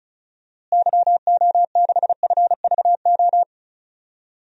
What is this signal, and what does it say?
Morse code, heard as a keyed tone, about 36 wpm, 700 Hz. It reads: YO6FVO
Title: DXCC-100 Call Signs - 6 Characters - Repeated 3x - 35wpm